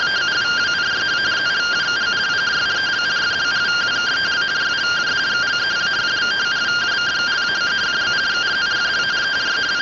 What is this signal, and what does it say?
Two-tone teleprinter signal, about 170 Hz shift.